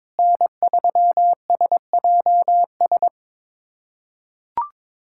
Morse code, heard as a keyed tone, about 22 wpm, 700 Hz. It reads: N3SJS
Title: USA Callsign DE Callsign 22wpm